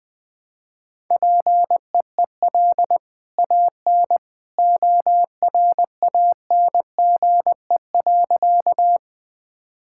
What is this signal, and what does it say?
Morse code, heard as a keyed tone, about 21 wpm, 700 Hz. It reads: PEEL AN ORANGE.
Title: Easy Sentences with Hints - 20wpm